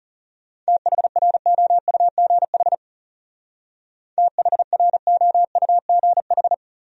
Morse code, heard as a keyed tone, about 34 wpm, 700 Hz. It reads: THROUGH THROUGH
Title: Speed-Racing Top 200 Words 30wpm